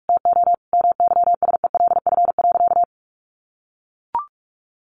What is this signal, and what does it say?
Morse code, heard as a keyed tone, about 44 wpm, 700 Hz. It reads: TO MYSELF.
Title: Sentences from Top 1300 Words 45wpm